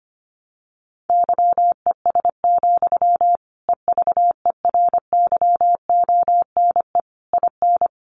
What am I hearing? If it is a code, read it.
Morse code, heard as a keyed tone, about 25 wpm, 700 Hz. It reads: YES, EVERYONE IN
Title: Sentences from Top 800 Words 25wpm